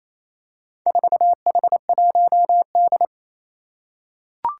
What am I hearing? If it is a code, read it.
Morse code, heard as a keyed tone, about 28 wpm, 700 Hz. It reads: 4H1D
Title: Four Challenging Letter-Number Combinations 28wpm